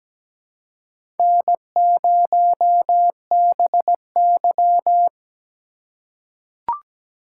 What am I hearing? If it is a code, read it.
Morse code, heard as a keyed tone, about 17 wpm, 700 Hz. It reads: N0BY
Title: USA Call Signs - 4 Characters - 3x Character Spacing - 17wpm